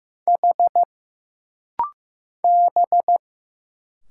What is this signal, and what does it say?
Morse code, heard as a keyed tone, about 15 wpm, 700 Hz. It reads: H B
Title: Single Letters - Rapid-Fire 15wpm